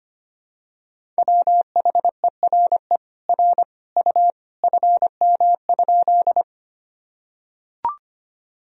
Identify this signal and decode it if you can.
Morse code, heard as a keyed tone, about 25 wpm, 700 Hz. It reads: WHERE R U FM?
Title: Sentences from Top 200 Words and Abbreviations 25wpm